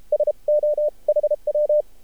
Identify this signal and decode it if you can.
Morse code, heard as a keyed tone, about 32 wpm, spaced as for 27 wpm, 600 Hz. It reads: SOHW
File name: archived / CRSnet